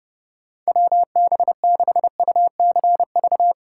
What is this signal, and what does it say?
Morse code, heard as a keyed tone, about 30 wpm, 700 Hz. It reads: WB6UCV
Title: USA Call Signs - 6 Characters - 4x Character Spacing - 30wpm